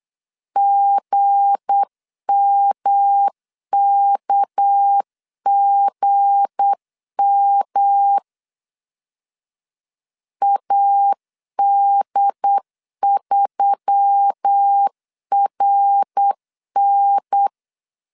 Morse code, heard as a keyed tone, about 8 wpm, 800 Hz. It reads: GMKGM AD3RN
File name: LEZIONI / 7 GF9Z / 7 G 8wpm 3118